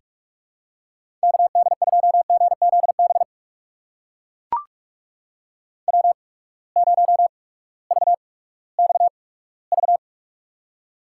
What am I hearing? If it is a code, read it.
Morse code, heard as a keyed tone, about 44 wpm, spaced as for 20 wpm, 700 Hz. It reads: KD2GZB W0VXV
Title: USA Call Signs - 8x Character Spacing - 45wpm